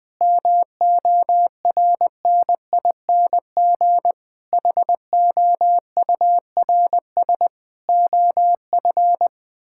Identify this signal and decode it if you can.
Morse code, heard as a keyed tone, about 20 wpm, 700 Hz. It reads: MORNING HOURS OF
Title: Sets of 6 Words - 20wpm